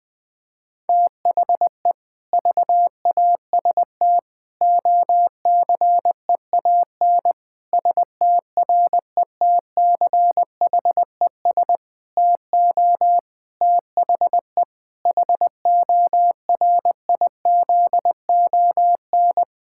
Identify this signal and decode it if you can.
Morse code, heard as a keyed tone, about 20 wpm, 700 Hz. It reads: THE VAST OCEAN STRETCHES TO THE HORIZON
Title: Difficult Sentences with Hints - 20wpm